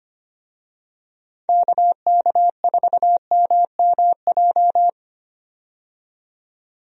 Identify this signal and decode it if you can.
Morse code, heard as a keyed tone, about 25 wpm, 700 Hz. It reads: KK4MMJ
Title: USA Call Signs - 6 Characters - 25wpm